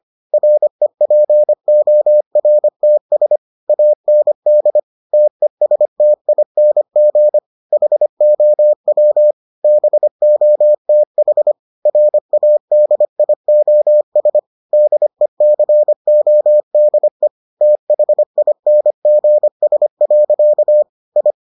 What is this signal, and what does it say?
Morse code, heard as a keyed tone, about 25 wpm, 600 Hz. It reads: REPORTS AND TESTING HOW BOTH RADIOS DECODE THINGS. I